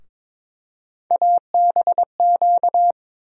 Morse code, heard as a keyed tone, about 22 wpm, 700 Hz. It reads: ABQ
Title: USA Call Sign Postfixes - 3 Letters 22wpm